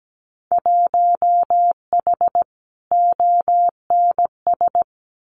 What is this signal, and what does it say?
Morse code, heard as a keyed tone, about 17 wpm, 700 Hz. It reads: 1H ONS
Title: Winter Field Day Exchanges 17wpm